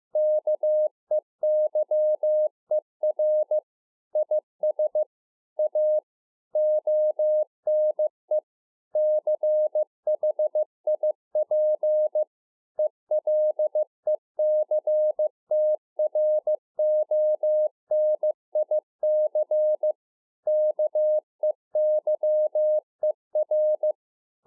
Morse code, heard as a keyed tone, about 15 wpm, 610 Hz. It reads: KEYER IS A ONE CHIP ELECTRONIC KEYER
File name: 2011-04-mp3 / A SMALL TRANCEIVER 2 15WPM